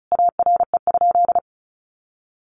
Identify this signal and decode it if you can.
Morse code, heard as a keyed tone, about 34 wpm, 700 Hz. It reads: ARE?
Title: A Sound Of Thunder 35wpm